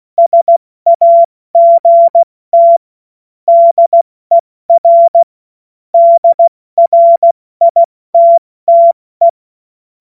Morse code, heard as a keyed tone, about 16 wpm, 680 Hz. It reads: SAGT DER DRITTE
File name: Morsekurs / Witze / Witz 01. Februar 2022 70 F80